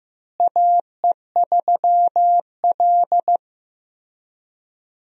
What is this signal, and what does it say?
Morse code, heard as a keyed tone, about 15 wpm, 700 Hz. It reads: AE3L
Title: USA Call Signs - 4 Characters - Repeated 3x - 15wpm